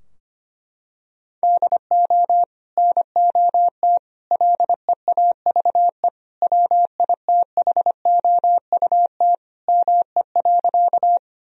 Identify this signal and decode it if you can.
Morse code, heard as a keyed tone, about 25 wpm, 700 Hz. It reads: DO NOT LEAVE WITHOUT ME.
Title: Sentences from Top 1000 Words - Review Part 3 - 25wpm